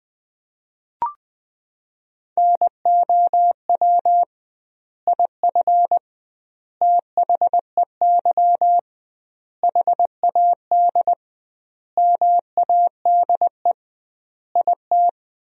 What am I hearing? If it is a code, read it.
Morse code, heard as a keyed tone, about 20 wpm, 700 Hz. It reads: NOW IF THEY HAD MADE IT
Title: 2x Word-Spacing - Sentences from Top 500 Words - 20wpm